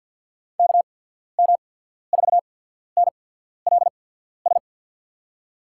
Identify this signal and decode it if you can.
Morse code, heard as a keyed tone, about 54 wpm, 700 Hz. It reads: K M 4 N L S
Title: USA Call Signs - 6 Characters - 8x Character Spacing - 50wpm